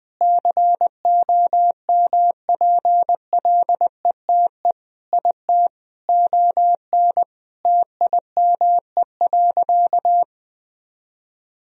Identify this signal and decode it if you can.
Morse code, heard as a keyed tone, about 20 wpm, 700 Hz. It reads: COMPLETE IT ON TIME.
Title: Sentences from Top 1000 Words - Review Part 2 - 20wpm